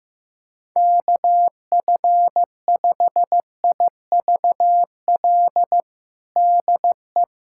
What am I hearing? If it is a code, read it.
Morse code, heard as a keyed tone, about 15 wpm, 700 Hz. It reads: KF5IVL DE
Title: USA Callsign DE Callsign - 6 Characters - 15wpm